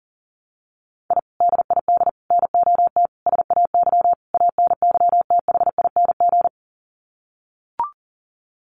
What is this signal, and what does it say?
Morse code, heard as a keyed tone, about 40 wpm, 700 Hz. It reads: I DID NOT SAY ANYTHING
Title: Sets of 5 Words Encore2 40wpm